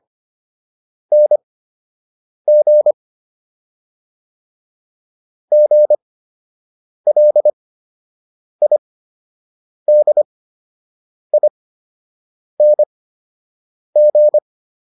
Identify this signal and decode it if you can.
Morse code, heard as a keyed tone, about 25 wpm, spaced as for 7 wpm, 600 Hz. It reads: NG GLIDING